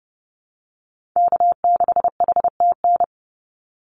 Morse code, heard as a keyed tone, about 30 wpm, 700 Hz. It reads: K6HTN
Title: Call Signs Repeated 3x - 30wpm